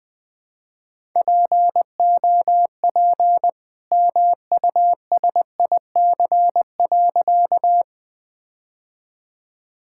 Morse code, heard as a keyed tone, about 20 wpm, 700 Hz. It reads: POP MUSIC.
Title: All-Age Appropriate Jokes - 20wpm